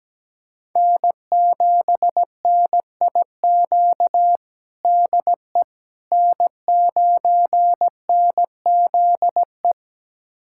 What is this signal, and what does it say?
Morse code, heard as a keyed tone, about 17 wpm, 700 Hz. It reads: N7NIQ DE N9NZE
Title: USA Callsign DE Callsign - 5 Characters - 17wpm